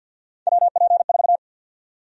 Morse code, heard as a keyed tone, about 51 wpm, 700 Hz. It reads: WP4
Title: USA Call Sign Prefixes - 50wpm